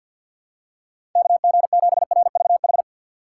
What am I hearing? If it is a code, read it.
Morse code, heard as a keyed tone, about 50 wpm, 700 Hz. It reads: KG7RVH
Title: USA Call Signs - 6 Characters - 6x Character Spacing - 50wpm